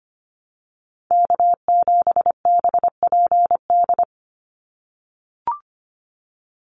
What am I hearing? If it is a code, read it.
Morse code, heard as a keyed tone, about 25 wpm, 700 Hz. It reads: K7BPD
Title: USA Call Signs 25wpm